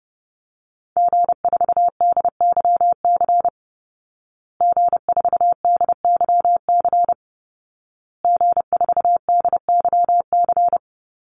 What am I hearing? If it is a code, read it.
Morse code, heard as a keyed tone, about 30 wpm, 700 Hz. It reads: G4DYC G4DYC G4DYC
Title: Call Signs Repeated 2x - 30wpm